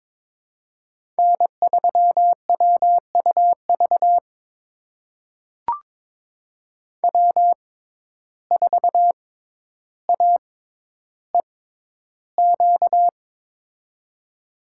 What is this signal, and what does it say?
Morse code, heard as a keyed tone, about 22 wpm, spaced as for 11 wpm, 700 Hz. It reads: N3WUV W4AEQ
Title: USA Call Signs - 5 Characters - 6x Character Spacing - 22wpm